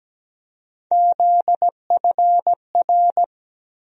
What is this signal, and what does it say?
Morse code, heard as a keyed tone, about 17 wpm, 700 Hz. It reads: ZFR
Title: USA Call Sign Postfixes - 3 Letters 17wpm